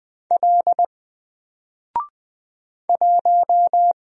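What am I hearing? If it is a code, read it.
Morse code, heard as a keyed tone, about 20 wpm, 700 Hz. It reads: L 1
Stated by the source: Single letter and number Morse drill at 20 wpm, from N4LCD.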